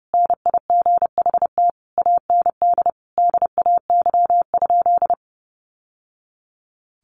Morse code, heard as a keyed tone, about 30 wpm, 700 Hz. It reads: NIGHT AND DAY?
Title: Sentences from Top 1100 Words 30wpm